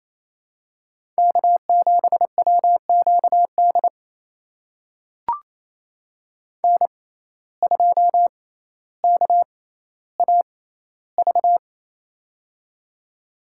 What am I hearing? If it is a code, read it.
Morse code, heard as a keyed tone, about 28 wpm, spaced as for 14 wpm, 700 Hz. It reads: K7WQD N2KAV
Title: USA Call Signs - 5 Characters - 6x Character Spacing - 28wpm